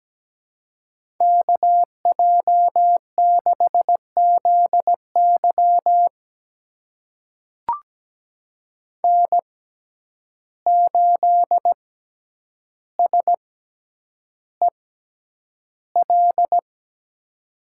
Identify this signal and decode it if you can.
Morse code, heard as a keyed tone, about 17 wpm, spaced as for 9 wpm, 700 Hz. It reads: KJ6ZY N8SEL
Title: USA Call Signs - 5 Characters - 6x Character Spacing - 17wpm